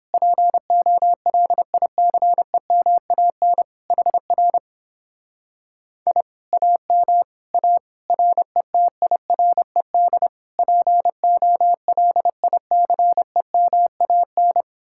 Morse code, heard as a keyed tone, about 29 wpm, 700 Hz. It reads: POLICEMAN HR I AM A RETIRED POLICEMAN